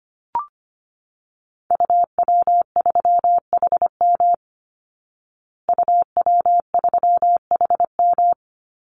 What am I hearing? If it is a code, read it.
Morse code, heard as a keyed tone, about 25 wpm, 700 Hz. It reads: UW3HM UW3HM
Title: Call Signs Repeated 3x - 25wpm